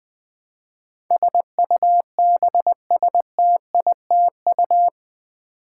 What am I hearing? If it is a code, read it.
Morse code, heard as a keyed tone, about 20 wpm, 700 Hz. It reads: SUBSTITU
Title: Build A Word - 4 Letters At A Time - 12-Letter Words - 20wpm